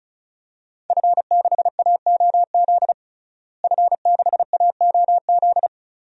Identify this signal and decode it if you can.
Morse code, heard as a keyed tone, about 34 wpm, 700 Hz. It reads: F6AOZ F6AOZ
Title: DXCC-100 Call Signs - 5 Characters - Repeated 3x - 35wpm